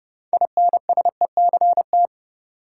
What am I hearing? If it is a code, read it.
Morse code, heard as a keyed tone, about 30 wpm, 700 Hz. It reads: INSECT
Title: Six Letter Words - 30wpm